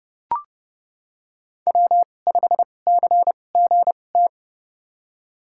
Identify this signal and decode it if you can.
Morse code, heard as a keyed tone, about 30 wpm, spaced as for 22 wpm, 700 Hz. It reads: W5CGT
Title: USA Call Signs - 5 Characters - 2x Character Spacing - 30wpm